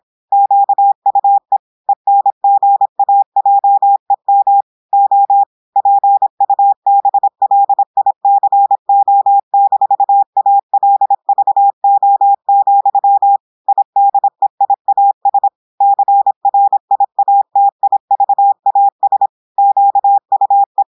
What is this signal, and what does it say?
Morse code, heard as a keyed tone, about 26 wpm, 800 Hz. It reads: QUE ENGAJEM O PUBLICO-ALVO, IDEIAS CRIATIVAS QUE
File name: portugais texte en clair 2 vitesse 26 mots minutes